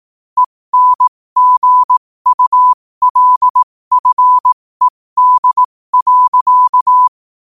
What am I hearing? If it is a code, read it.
Morse code, heard as a keyed tone, about 18 wpm, spaced as for 15 wpm, 1000 Hz. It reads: ENGULFED.
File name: qotdfiles / 15wpm